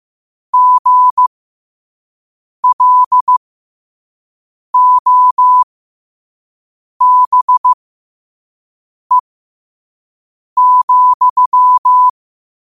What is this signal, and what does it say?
Morse code, heard as a keyed tone, about 15 wpm, spaced as for 5 wpm, 1000 Hz. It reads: GLOBE,
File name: qotdfiles / 5wpm